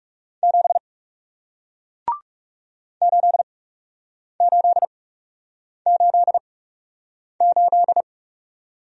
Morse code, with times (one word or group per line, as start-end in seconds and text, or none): 0.4-0.8 7
3.0-8.0 8888